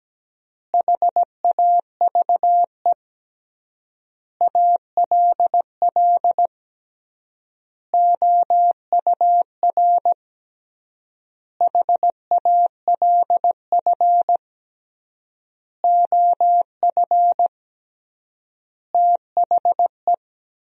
Morse code, heard as a keyed tone, about 17 wpm, 700 Hz. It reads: HAVE ALL OUR HALF OF THE
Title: 3x Word-Spacing - Sentences from Top 400 Words - 17wpm